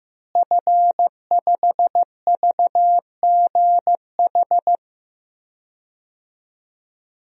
Morse code, heard as a keyed tone, about 16 wpm, 700 Hz. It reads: F5VGH
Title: DXCC-100 Call Signs - 5 Characters - Repeated 3x - 15wpm